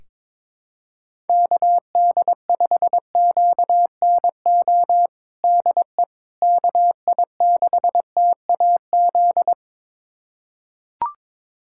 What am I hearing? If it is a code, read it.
Morse code, heard as a keyed tone, about 22 wpm, 700 Hz. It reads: KD5QNO DE KI6TAZ
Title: USA Callsign DE Callsign 22wpm